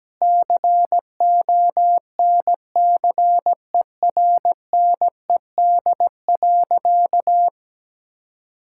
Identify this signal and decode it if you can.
Morse code, heard as a keyed tone, about 17 wpm, 700 Hz. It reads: CONCERNED.
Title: Sentences from Top 1000 Words - Review Part 6 - 17wpm